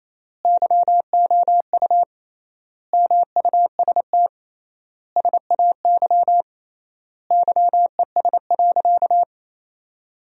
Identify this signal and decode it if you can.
Morse code, heard as a keyed tone, about 28 wpm, 700 Hz. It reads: YOU MUST SAY YES.
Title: 3x Word-Spacing - Sentences from Top 500 Words - 28wpm